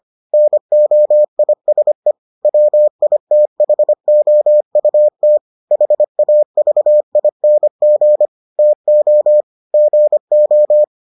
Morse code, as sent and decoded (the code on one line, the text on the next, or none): -. --- .. ... / .-- .. - .... --- ..- - / .... .- ...- .. -. --. / - --- / --. ---
NOISE WITHOUT HAVING TO GO